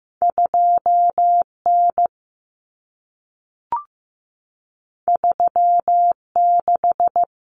Morse code, heard as a keyed tone, about 15 wpm, 700 Hz. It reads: 2N 36
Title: Common Cut Numbers - 2-Digit 15wpm